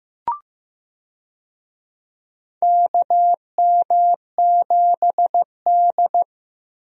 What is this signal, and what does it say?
Morse code, heard as a keyed tone, about 15 wpm, 700 Hz. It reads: KM7D
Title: USA Call Signs - 4 Characters - Repeated 2x - 15wpm